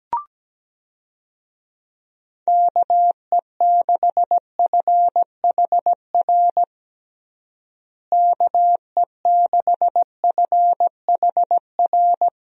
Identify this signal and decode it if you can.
Morse code, heard as a keyed tone, about 17 wpm, 700 Hz. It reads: KE6FHR KE6FHR
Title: USA Call Signs - 6 Characters - Repeated 2x - 17wpm